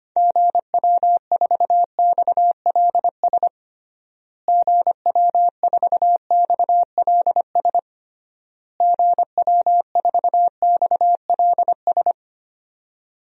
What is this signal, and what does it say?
Morse code, heard as a keyed tone, about 25 wpm, 700 Hz. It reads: GW4XLS GW4XLS GW4XLS
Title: DXCC-100 Call Signs - 6 Characters - Repeated 3x - 25wpm